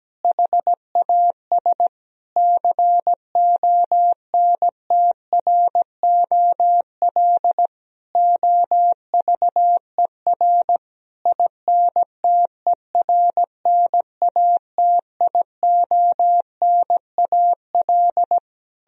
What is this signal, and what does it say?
Morse code, heard as a keyed tone, about 17 wpm, 700 Hz. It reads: HAS CONTROL OVER INTERNATIONAL
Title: Sentences from Top 1000 Words - Review Part 4 - 17wpm